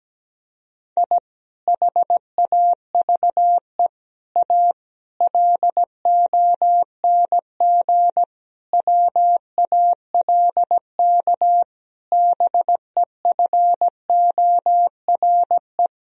Morse code, heard as a keyed tone, about 17 wpm, 700 Hz. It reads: I HAVE A LONG WALK BEFORE
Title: Sentences from Top 1000 Words - Review Part 1 - 17wpm